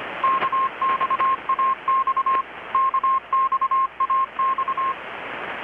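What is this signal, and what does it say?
Morse code, heard as a keyed tone, about 25 wpm, 1060 Hz. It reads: KXAX KXAX
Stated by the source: Morse from KPH's Night of Nights XX maritime radiotelegraphy event, received in Chiapas, Mexico about 3,800 km away on a Kenwood R-600 with a grounded 10 m long-wire antenna.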